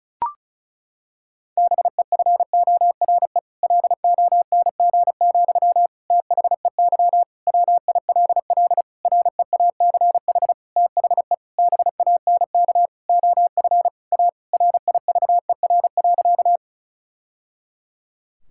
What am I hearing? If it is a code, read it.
Morse code, heard as a keyed tone, about 36 wpm, 700 Hz. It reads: BEFORE LONG, THEY WILL REACH THE BANK OF A RIVER.
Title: Sentences from Top 1000 Words - Review Part 6 - 35wpm